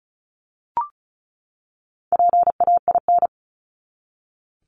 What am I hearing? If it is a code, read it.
Morse code, heard as a keyed tone, about 34 wpm, 700 Hz. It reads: PAIN
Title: Top 1501-1750 Words 35wpm